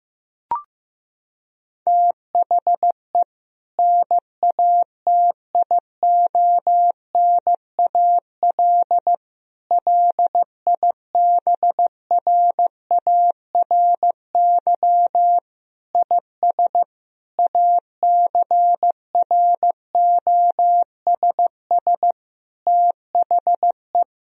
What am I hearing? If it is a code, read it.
Morse code, heard as a keyed tone, about 15 wpm, 700 Hz. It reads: THE NATIONAL LIBRARY IS ACROSS THE
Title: Sentences from Top 1000 Words - Review Part 4 - 15wpm